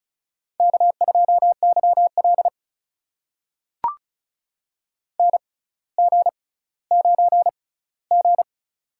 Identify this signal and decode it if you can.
Morse code, heard as a keyed tone, about 36 wpm, spaced as for 18 wpm, 700 Hz. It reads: K2YL NG9G
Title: USA Call Signs - 4 Characters - 6x Character Spacing - 35wpm